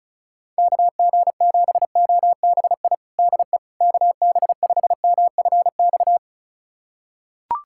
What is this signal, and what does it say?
Morse code, heard as a keyed tone, about 36 wpm, 700 Hz. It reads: KG7OBI DE KB5MFX
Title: USA Callsign DE Callsign - 6 Characters - 35wpm